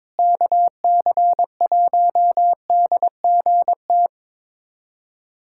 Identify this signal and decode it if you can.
Morse code, heard as a keyed tone, about 22 wpm, 700 Hz. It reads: KC1DGT
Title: USA Callsign DE Callsign - 6 Characters - 22wpm